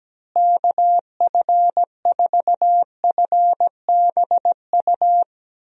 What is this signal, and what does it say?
Morse code, heard as a keyed tone, about 17 wpm, 700 Hz. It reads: KF4FBU
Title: USA Call Signs - 6 Characters - Repeated 2x - 17wpm